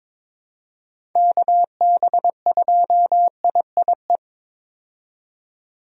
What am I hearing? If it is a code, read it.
Morse code, heard as a keyed tone, about 22 wpm, 700 Hz. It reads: KB2IIE
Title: USA Call Signs - 6 Characters - 22wpm